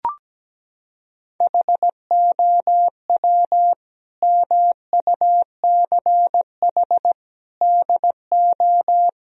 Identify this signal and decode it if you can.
Morse code, heard as a keyed tone, about 17 wpm, 700 Hz. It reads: HOW MUCH DO
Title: Sentences from Top 1000 Words - Review Part 3 - 17wpm